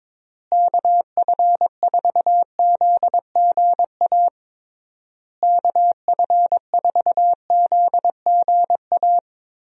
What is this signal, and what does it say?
Morse code, heard as a keyed tone, about 22 wpm, 700 Hz. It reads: KF4ZGA KF4ZGA
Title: USA Call Signs - 6 Characters - Repeated 3x - 22wpm